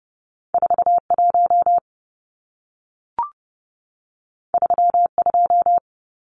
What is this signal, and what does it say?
Morse code, heard as a keyed tone, about 30 wpm, 700 Hz. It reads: 41 32
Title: Numbers - 2-Digit 30wpm